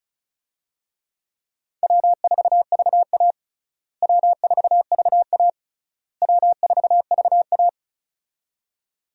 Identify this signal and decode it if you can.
Morse code, heard as a keyed tone, about 35 wpm, 700 Hz. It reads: W4VA W4VA W4VA
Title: USA Call Signs - 4 Characters - Repeated 3x - 35wpm